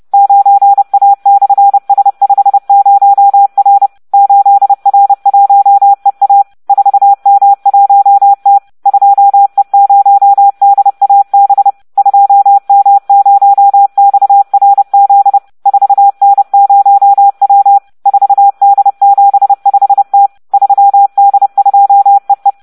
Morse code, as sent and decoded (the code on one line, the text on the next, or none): ----. .- -..-. ... ..... ----- .-. / ---.. .-. .---- . .- / ....- -- .---- - / ..--- . ----- -.. .- -... / ..--- -- ----- -..- .-. --.. / ....- -. ----- .-- / ....- -.. --... ..... - / ...-- -.. ..--- . .
9A/S50R 8R1EA 4M1T 2E0DAB 2M0XRZ 4N0W 4D75T 3D2EE